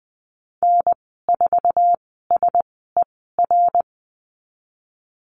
Morse code, heard as a keyed tone, about 20 wpm, spaced as for 15 wpm, 700 Hz. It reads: N4SER
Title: 055 Lesson 10g - Letter r - Call Signs 20wpm